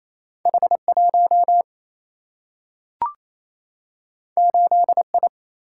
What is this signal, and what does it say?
Morse code, heard as a keyed tone, about 28 wpm, 700 Hz. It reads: H 1 8 I
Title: Single-Letter Single-Number 28wpm